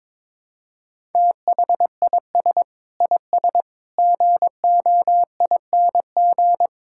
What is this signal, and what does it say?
Morse code, heard as a keyed tone, about 22 wpm, 700 Hz. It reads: THIS IS GOING
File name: practice / 22wpm / Sentences with 4-Word Phrases - 22wpm